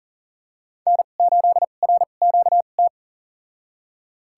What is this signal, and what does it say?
Morse code, heard as a keyed tone, about 40 wpm, spaced as for 29 wpm, 700 Hz. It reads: N8RQT
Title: USA Call Signs - 2x Character Spacing - 40wpm